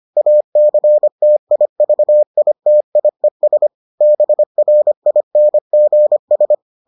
Morse code, heard as a keyed tone, about 25 wpm, 600 Hz. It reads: ACTIVITIES BRINGS